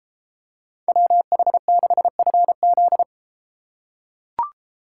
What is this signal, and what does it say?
Morse code, heard as a keyed tone, about 32 wpm, 700 Hz. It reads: WH6FZ